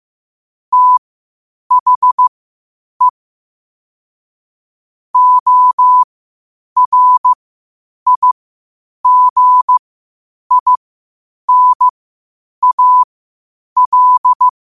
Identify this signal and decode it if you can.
Morse code, heard as a keyed tone, about 15 wpm, spaced as for 8 wpm, 1000 Hz. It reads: THE ORIGINAL
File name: qotdfiles / 8wpm